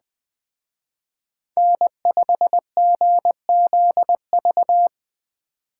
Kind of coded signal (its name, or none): Morse code